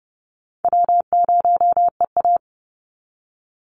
Morse code, heard as a keyed tone, about 30 wpm, 700 Hz. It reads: W0EA
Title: Easy Call Signs - 30wpm